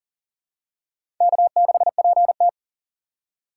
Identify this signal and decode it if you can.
Morse code, heard as a keyed tone, about 39 wpm, 700 Hz. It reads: K6PT
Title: USA Call Signs - 4 Characters - 2x Character Spacing - 40wpm